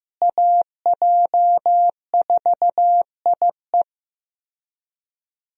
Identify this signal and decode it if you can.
Morse code, heard as a keyed tone, about 15 wpm, 700 Hz. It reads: AJ4IE
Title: USA Call Signs - 5 Characters - Repeated 3x - 15wpm